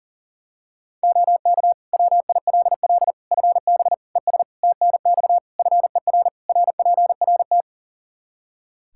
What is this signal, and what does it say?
Morse code, heard as a keyed tone, about 37 wpm, 700 Hz. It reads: OK WILL FB ES TNX FER RPRT
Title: Practice QSO - Exchange 2 - Info - 40wpm